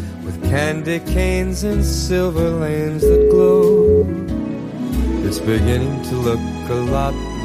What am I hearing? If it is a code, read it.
Ringback tone (Europe, 425 Hz), with speech.